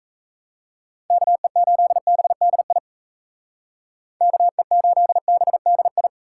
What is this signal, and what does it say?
Morse code, heard as a keyed tone, about 34 wpm, 700 Hz. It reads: KE8BDI KE8BDI
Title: Speed Racing - USA Call Signs 28wpm